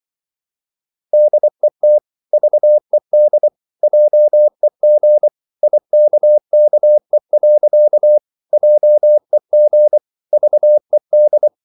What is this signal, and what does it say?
Morse code, heard as a keyed tone, about 24 wpm, 600 Hz. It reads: DET VED JEG IKKE. JEG VED
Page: Ferie 24 wpm